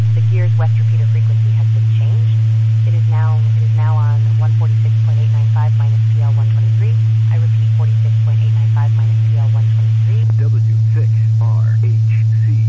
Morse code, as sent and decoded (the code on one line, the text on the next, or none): none